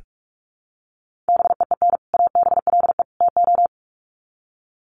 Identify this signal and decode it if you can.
Morse code, heard as a keyed tone, about 44 wpm, 700 Hz. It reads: BEEN ABLE TO
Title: Sets of 3 Words 45wpm